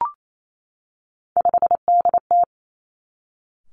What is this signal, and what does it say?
Morse code, heard as a keyed tone, about 28 wpm, 700 Hz. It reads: 5DT
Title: Three Challenging Letter-Number Combinations 28wpm